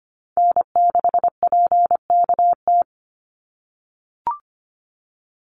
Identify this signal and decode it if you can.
Morse code, heard as a keyed tone, about 25 wpm, 700 Hz. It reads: N6PKT
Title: NA SOTA Chaser Callsigns 25wpm